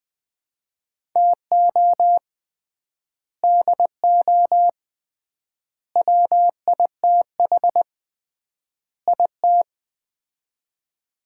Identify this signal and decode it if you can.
Morse code, heard as a keyed tone, about 20 wpm, 700 Hz. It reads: TO DO WITH IT
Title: 3x Word-Spacing - Sets of 4 Words - 20wpm